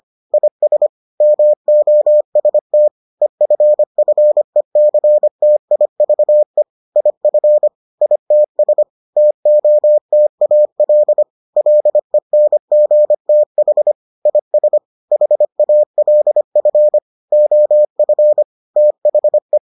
Morse code, heard as a keyed tone, about 25 wpm, 600 Hz. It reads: IS MOST EFFECTIVE IF ITS TOTAL LENGTH IS HALF OF THE